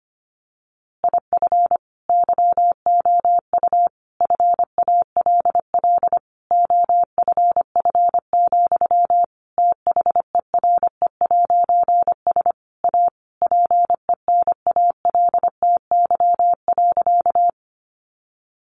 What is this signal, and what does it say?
Morse code, heard as a keyed tone, about 25 wpm, 700 Hz. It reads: IF YOU FALL OFF, THERE'S A PENALTY.
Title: A Sound Of Thunder 25wpm